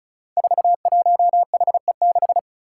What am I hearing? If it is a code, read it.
Morse code, heard as a keyed tone, about 34 wpm, 700 Hz. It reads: 41HE6
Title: Five Challenging Letter-Number Combinations 35wpm